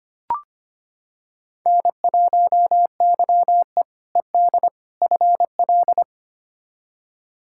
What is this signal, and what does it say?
Morse code, heard as a keyed tone, about 25 wpm, 700 Hz. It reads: N1YE ED FL
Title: K1USN SST QSO Practice - 25wpm